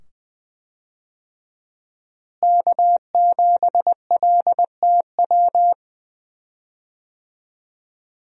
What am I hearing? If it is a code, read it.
Morse code, heard as a keyed tone, about 20 wpm, 700 Hz. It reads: K7LTW
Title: USA Call Signs - 5 Characters - Repeated 2x - 20wpm